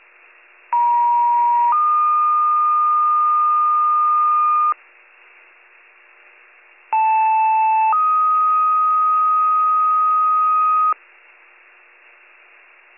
Two-tone page, heard as a one-second tone, then a three-second tone, about 950 then 1220 Hz, about 880 then 1250 Hz. It is heard from 0.7 to 10.9 s.